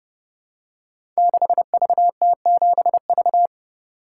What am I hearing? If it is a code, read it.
Morse code, heard as a keyed tone, about 30 wpm, 700 Hz. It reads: 6VT7V
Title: Code Groups - 5 Letter:Numbers - 30wpm